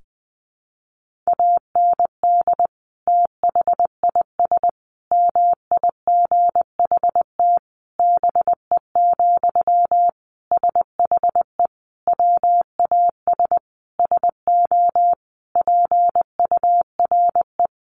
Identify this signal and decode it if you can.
Morse code, heard as a keyed tone, about 19 wpm, 700 Hz. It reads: AND THIS MIGHT BE, SHE WAS SO PURE
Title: Sentences from Top 1500 Words 20wpm